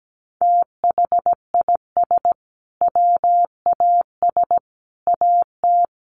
Morse code, heard as a keyed tone, about 17 wpm, 700 Hz. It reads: THIS WAS AT
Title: Sentences from Top 1500 Words 17wpm